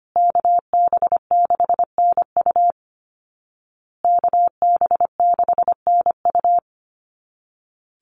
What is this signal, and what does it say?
Morse code, heard as a keyed tone, about 25 wpm, 700 Hz. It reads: KB6NU KB6NU
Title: Call Signs Repeated 2x - 25wpm